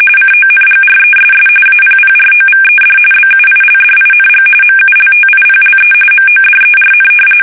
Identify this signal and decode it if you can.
RTTY, about 45.45 baud. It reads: CONGRESS APPROVED BUILDING MULTIPLE MODERN
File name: fsk850 / usnavy1